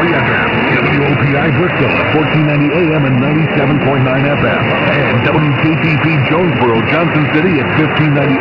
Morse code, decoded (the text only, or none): none